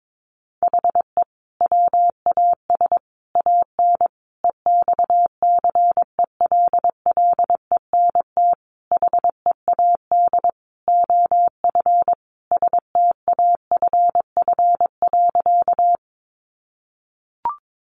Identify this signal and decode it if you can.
Morse code, heard as a keyed tone, about 22 wpm, 700 Hz. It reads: HE WAS AN EXCELLENT HEAD OF STAFF.
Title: Sentences from Top 1200 Words 22wpm